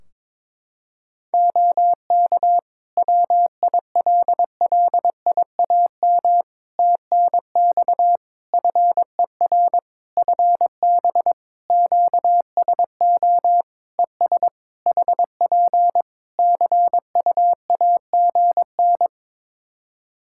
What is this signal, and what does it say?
Morse code, heard as a keyed tone, about 22 wpm, 700 Hz. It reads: OK WILLIAM TNX FER FB QSO ES HP CUAGN
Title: Practice QSO - Exchange 4 - Ending - 22wpm